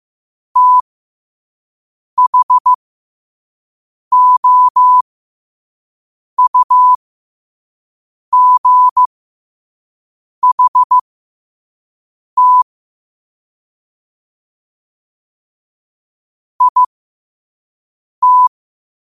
Morse code, heard as a keyed tone, about 15 wpm, spaced as for 5 wpm, 1000 Hz. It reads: THOUGHT IT
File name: qotdfiles / 5wpm